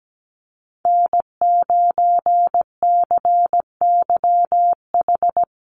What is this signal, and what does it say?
Morse code, heard as a keyed tone, about 17 wpm, 700 Hz. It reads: N9CYH
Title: USA Call Signs Repeated 3x 17wpm